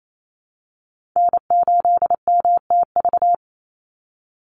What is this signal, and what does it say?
Morse code, heard as a keyed tone, about 28 wpm, 700 Hz. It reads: N8MTV
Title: Call Signs Repeated 3x - 28wpm